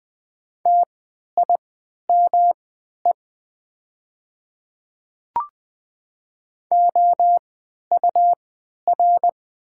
Morse code, heard as a keyed tone, about 20 wpm, spaced as for 11 wpm, 700 Hz. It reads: TIME OUR
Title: 3x Character Spacing - Top 200 Words - 20wpm